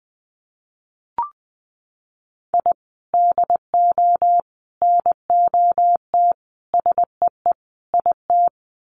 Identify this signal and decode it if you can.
Morse code, heard as a keyed tone, about 20 wpm, 700 Hz. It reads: I DO NOT SEE IT
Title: Sets of 5 Words Encore2 20wpm